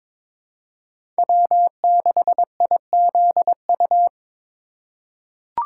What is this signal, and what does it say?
Morse code, heard as a keyed tone, about 21 wpm, 700 Hz. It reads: W6IZU
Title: USA Call Signs - 4x Character Spacing - 22wpm